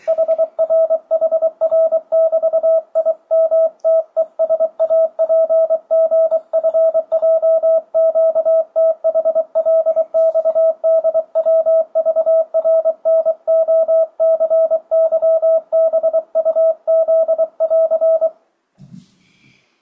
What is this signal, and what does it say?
Morse code, heard as a keyed tone, about 23 wpm, 650 Hz. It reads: HRHR<BT>IMTESAPGFJQTHLXDWVRNOCYBUZ<AR>
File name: 2025cw / speedup / 24wpm